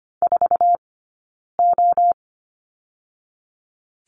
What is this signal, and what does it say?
Morse code, heard as a keyed tone, about 25 wpm, 700 Hz. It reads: 4 O
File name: Lesson 17d - Letter u - Two Characters 25wpm